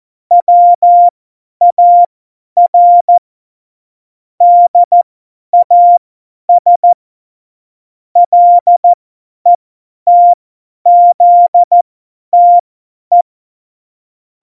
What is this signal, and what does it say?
Morse code, heard as a keyed tone, about 14 wpm, spaced as for 10 wpm, 700 Hz. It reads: WAR DAS LETZTE
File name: Morsekurs / Witze / Witz 18 August 2020, 50 F70